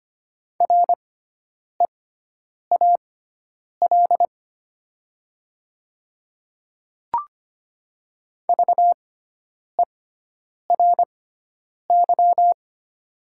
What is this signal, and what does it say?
Morse code, heard as a keyed tone, about 25 wpm, spaced as for 9 wpm, 700 Hz. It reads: REAL VERY
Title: 6x Character Spacing - Top 300 Words - 25wpm